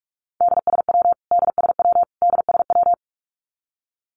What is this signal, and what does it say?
Morse code, heard as a keyed tone, about 44 wpm, 700 Hz. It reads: DSW DSW DSW
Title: Abbreviations Repeated 3x - 45wpm